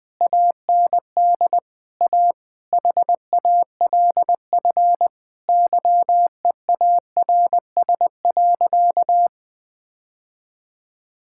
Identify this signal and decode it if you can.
Morse code, heard as a keyed tone, about 20 wpm, 700 Hz. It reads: AND A HALF YEARS.
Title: Sentences with 5-Word Phrases - 20wpm